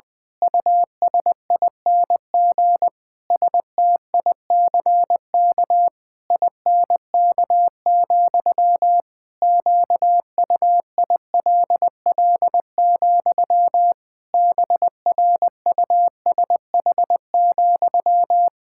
Morse code, heard as a keyed tone, about 20 wpm, 700 Hz. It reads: USING STICK INK, QUILL, BRUSH,
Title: Pangram Challenge v2 - 20wpm